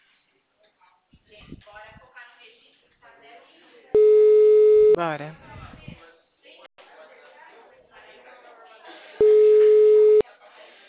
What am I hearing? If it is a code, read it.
Ringback tone (Europe, 425 Hz), with speech.